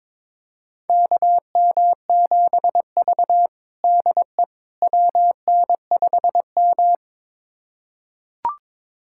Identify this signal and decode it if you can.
Morse code, heard as a keyed tone, about 23 wpm, 700 Hz. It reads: KM7V DE WN5M
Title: USA Callsign DE Callsign - 4 Characters - 22wpm